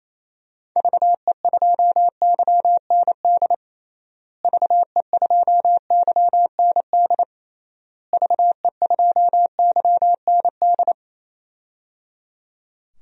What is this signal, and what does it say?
Morse code, heard as a keyed tone, about 28 wpm, 700 Hz. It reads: VE2YND VE2YND VE2YND
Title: DXCC-100 Call Signs - 6 Characters - Repeated 3x - 28wpm